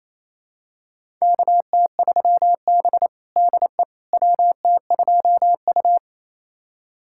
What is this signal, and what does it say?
Morse code, heard as a keyed tone, about 28 wpm, 700 Hz. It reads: KT3B DE WT2U
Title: USA Callsign DE Callsign - 4 Characters - 28wpm